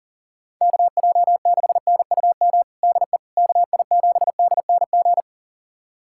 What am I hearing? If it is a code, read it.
Morse code, heard as a keyed tone, about 39 wpm, 700 Hz. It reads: KJ6NUM DE KI7DNG
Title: USA Callsign DE Callsign - 6 Characters - 40wpm